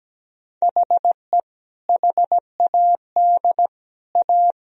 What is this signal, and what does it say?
Morse code, heard as a keyed tone, about 17 wpm, 700 Hz. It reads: HE HAD A
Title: Sentences from Top 1000 Words - Review Part 2 - 17wpm